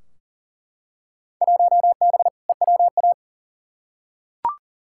Morse code, heard as a keyed tone, about 40 wpm, 700 Hz. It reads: 1B EWA
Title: Field Day Exchanges Encore - 40wpm